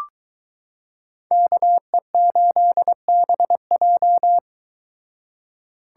Morse code, heard as a keyed tone, about 23 wpm, 700 Hz. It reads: KE8BJ